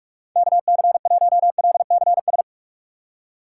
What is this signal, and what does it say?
Morse code, heard as a keyed tone, about 44 wpm, 700 Hz. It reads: KC1LCS
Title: USA Call Signs - 6 Characters - 8x Character Spacing - 45wpm